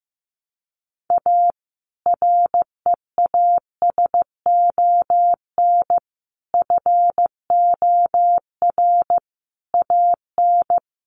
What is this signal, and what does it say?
Morse code, heard as a keyed tone, about 15 wpm, 700 Hz. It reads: A REASON FOR AN
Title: Sentences from Top 1300 Words 15wpm